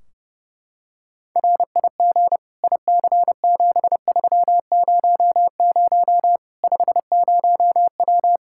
Morse code, heard as a keyed tone, about 32 wpm, 700 Hz. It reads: RIG IC7300 50W
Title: Practice QSO - Exchange 2 - Info - 30wpm